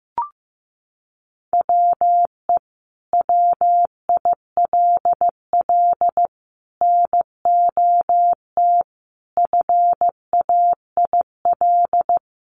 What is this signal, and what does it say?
Morse code, heard as a keyed tone, about 15 wpm, 700 Hz. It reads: WE WILL NOT FAIL
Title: Sentences from Top 1100 Words 15wpm